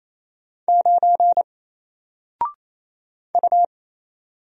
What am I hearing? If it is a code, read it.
Morse code, heard as a keyed tone, about 27 wpm, 700 Hz. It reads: 9 U
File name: Single-Letter-Number-28wpm